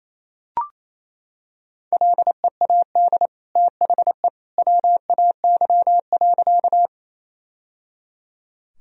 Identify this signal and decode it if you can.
Morse code, heard as a keyed tone, about 27 wpm, 700 Hz. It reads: LEAD THE WAY.
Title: Sentences from Top 1000 Words - Review Part 2 - 28wpm